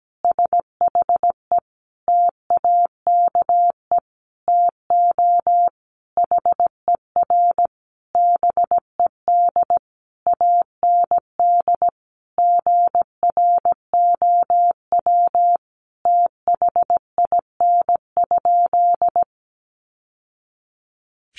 Morse code, heard as a keyed tone, about 17 wpm, 700 Hz. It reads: SHE TAKE TO HER BED AND GROW THIN?
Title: Sentences from Top 1200 Words 17wpm